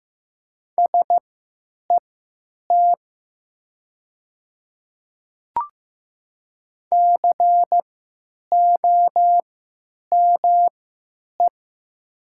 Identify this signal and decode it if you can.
Morse code, heard as a keyed tone, about 15 wpm, spaced as for 9 wpm, 700 Hz. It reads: SET COME
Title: 3x Character Spacing - Top 300 Words - 15wpm